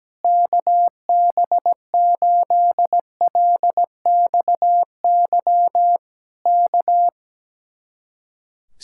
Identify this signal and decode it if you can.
Morse code, heard as a keyed tone, about 17 wpm, 700 Hz. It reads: KB8LXY K
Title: Speed-Racing Common QSO Phrases Encore - 17wpm